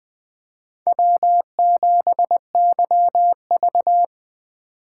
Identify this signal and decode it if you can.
Morse code, heard as a keyed tone, about 20 wpm, 700 Hz. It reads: W7YV
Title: USA Call Signs - 4 Characters - 3x Character Spacing - 20wpm